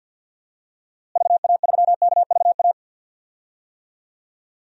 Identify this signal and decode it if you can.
Morse code, heard as a keyed tone, about 52 wpm, 700 Hz. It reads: VA3KVA
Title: DXCC-100 Call Signs - 6 Characters - Repeated 2x - 50wpm